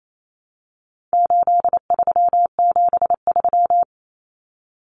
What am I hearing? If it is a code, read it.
Morse code, heard as a keyed tone, about 28 wpm, 700 Hz. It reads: 8373
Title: Numbers - 4-Digit 28wpm